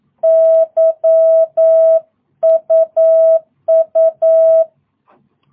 Morse code, heard as a keyed tone, about 9 wpm, 660 Hz. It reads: YUU
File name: jh8cbh / 2022cw